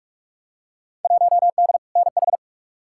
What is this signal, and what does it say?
Morse code, heard as a keyed tone, about 43 wpm, 700 Hz. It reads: 1D NH
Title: Field Day Exchanges - 45wpm